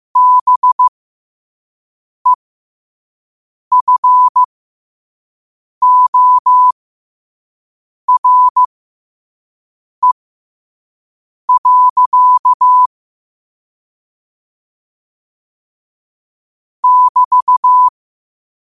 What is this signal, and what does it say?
Morse code, heard as a keyed tone, about 15 wpm, spaced as for 5 wpm, 1000 Hz. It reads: BEFORE. <BT>
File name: qotdfiles / 5wpm